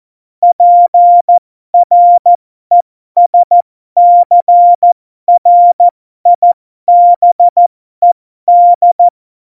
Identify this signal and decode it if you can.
Morse code, heard as a keyed tone, about 14 wpm, spaced as for 12 wpm, 700 Hz. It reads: PRESCRIBED
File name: Morsekurs / Witze